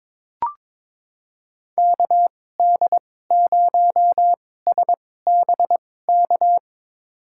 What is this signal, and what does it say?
Morse code, heard as a keyed tone, about 21 wpm, spaced as for 16 wpm, 700 Hz. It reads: KD0SBK
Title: USA Call Signs - 2x Character Spacing - 22wpm